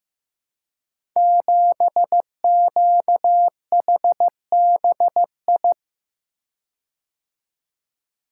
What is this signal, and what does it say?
Morse code, heard as a keyed tone, about 15 wpm, 700 Hz. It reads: 7QHBI
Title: Code Groups - 5 Letter:Numbers - 15wpm